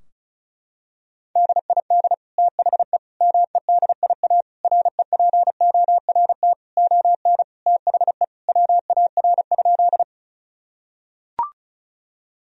Morse code, heard as a keyed tone, about 36 wpm, 700 Hz. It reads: DID THE MEDIA REPORT ON THE WAR?
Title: Sentences from Top 1000 Words - Review Part 4 - 35wpm